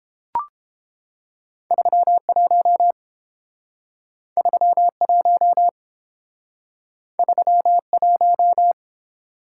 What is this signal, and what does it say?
Morse code, heard as a keyed tone, about 29 wpm, 700 Hz. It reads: 31 31 31